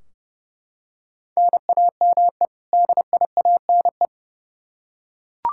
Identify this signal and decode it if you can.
Morse code, heard as a keyed tone, about 32 wpm, 700 Hz. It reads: NAME DIANE
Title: K1USN SST QSO Practice 30wpm